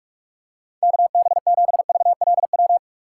Morse code, heard as a keyed tone, about 46 wpm, 700 Hz. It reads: KB7VLW
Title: USA Call Signs - 6 Characters - 2x Character Spacing - 45wpm